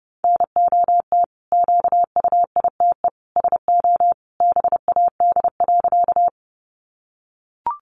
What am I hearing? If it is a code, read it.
Morse code, heard as a keyed tone, about 30 wpm, 700 Hz. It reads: NOT QUITE SO BAD.
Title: Sentences from Top 700 Words 30wpm